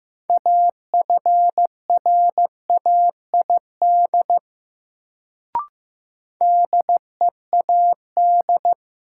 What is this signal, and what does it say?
Morse code, heard as a keyed tone, about 15 wpm, 700 Hz. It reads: AFRAID DEAD
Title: Sentences from Top 1500 Words - New Word Drill - 15wpm